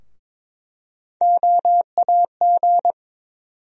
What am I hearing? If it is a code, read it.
Morse code, heard as a keyed tone, about 22 wpm, 700 Hz. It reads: OAG
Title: USA Call Sign Postfixes - 3 Letters 22wpm